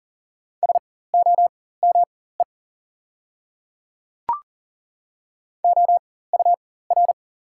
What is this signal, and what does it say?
Morse code, heard as a keyed tone, about 38 wpm, spaced as for 18 wpm, 700 Hz. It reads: SOME OUR
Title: 4x Character Spacing - Top 200 Words - 40wpm